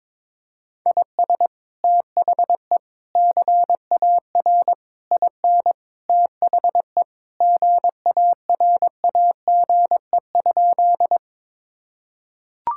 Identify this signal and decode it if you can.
Morse code, heard as a keyed tone, about 22 wpm, 700 Hz. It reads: IS THE CAR IN THE GARAGE?
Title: Sentences from Top 1600 Words 22wpm